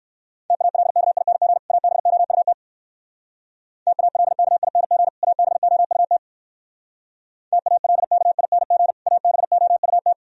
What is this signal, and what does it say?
Morse code, heard as a keyed tone, about 67 wpm, 690 Hz. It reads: TALKING ABOUT TALKING ABOUT TALKING ABOUT